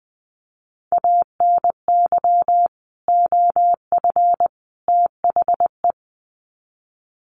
Morse code, heard as a keyed tone, about 20 wpm, 700 Hz. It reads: ANY OF THE
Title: Sets of 3 Words 20wpm